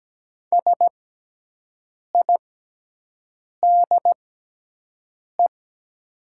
Morse code, heard as a keyed tone, about 17 wpm, spaced as for 6 wpm, 700 Hz. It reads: SIDE
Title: 6x Character Spacing - Top 100 Words - 17wpm